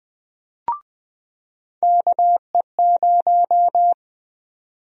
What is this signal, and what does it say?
Morse code, heard as a keyed tone, about 21 wpm, 700 Hz. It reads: KE0
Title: USA Call Sign Prefixes - 20wpm